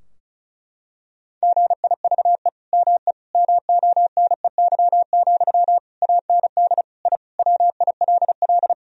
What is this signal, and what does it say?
Morse code, heard as a keyed tone, about 36 wpm, 700 Hz. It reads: GIVE ME MONEY, AND I WILL
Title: Sentences from Top 1000 Words - Review Part 2 - 35wpm